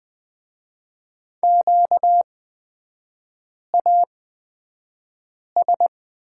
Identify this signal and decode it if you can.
Morse code, heard as a keyed tone, about 20 wpm, spaced as for 5 wpm, 700 Hz. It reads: QAS
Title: 236 Lesson 38e - Letter q - Three Characters 20wpm